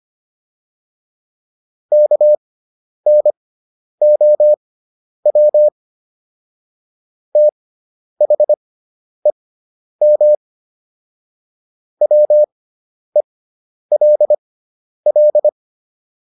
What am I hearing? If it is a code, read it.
Morse code, heard as a keyed tone, about 25 wpm, spaced as for 10 wpm, 600 Hz. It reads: KNOW THEM WELL